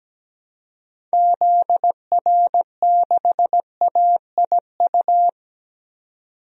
Morse code, heard as a keyed tone, about 17 wpm, 700 Hz. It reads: ZR6AIU
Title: DXCC-100 Call Signs - 6 Characters - 17wpm